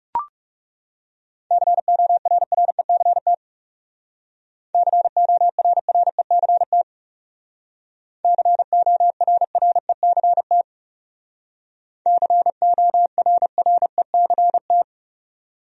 Morse code, heard as a keyed tone, about 36 wpm, 700 Hz. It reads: CORRECT CORRECT CORRECT CORRECT
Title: Speed-Racing Top 500 Words 30wpm